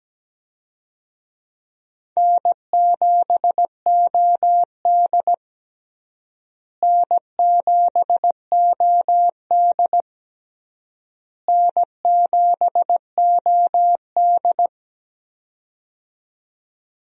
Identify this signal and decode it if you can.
Morse code, heard as a keyed tone, about 17 wpm, 700 Hz. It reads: N7OD N7OD N7OD
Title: USA Call Signs - 4 Characters - Repeated 3x - 17wpm